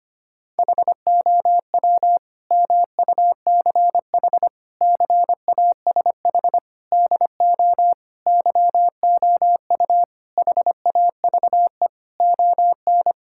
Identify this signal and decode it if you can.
Morse code, heard as a keyed tone, about 25 wpm, 700 Hz. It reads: HOW MUCH CASH DO YOU HAVE ON
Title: Sentences from Top 1000 Words - Review Part 5 - 25wpm